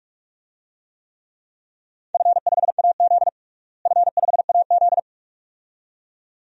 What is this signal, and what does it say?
Morse code, heard as a keyed tone, about 42 wpm, 700 Hz. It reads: F5AZ F5AZ
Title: DXCC-100 Call Signs - 4 Characters - Repeated 2x - 45wpm